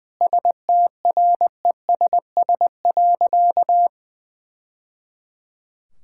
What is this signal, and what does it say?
Morse code, heard as a keyed tone, about 20 wpm, 700 Hz. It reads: STRESS.
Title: Sentences from Top 1000 Words - Review Part 6 - 20wpm